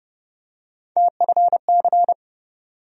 Morse code, heard as a keyed tone, about 30 wpm, 700 Hz. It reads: TFC
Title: USA Call Sign Postfixes - 3 Letters 30wpm